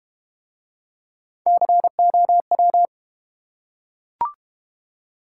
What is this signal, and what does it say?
Morse code, heard as a keyed tone, about 31 wpm, 700 Hz. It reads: COW
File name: Easy Words - Speed-Racing with Hints - 25wpm